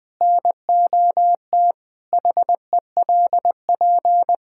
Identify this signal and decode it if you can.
Morse code, heard as a keyed tone, about 20 wpm, 700 Hz. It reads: NOT HELP
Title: Sentences from Top 1600 Words 20wpm